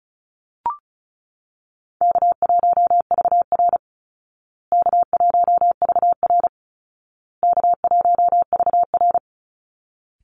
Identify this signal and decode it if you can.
Morse code, heard as a keyed tone, about 34 wpm, 700 Hz. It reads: K1VR K1VR K1VR
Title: Easy Call Signs Repeated 3x - 35wpm